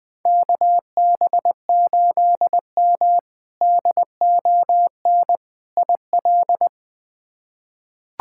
Morse code, heard as a keyed tone, about 20 wpm, 700 Hz. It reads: KB8M DON IL
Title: K1USN SST QSO Practice - 20wpm